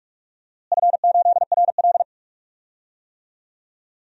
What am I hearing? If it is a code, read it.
Morse code, heard as a keyed tone, about 44 wpm, 700 Hz. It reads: F8RL
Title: DXCC-100 Call Signs - 4 Characters - Repeated 2x - 45wpm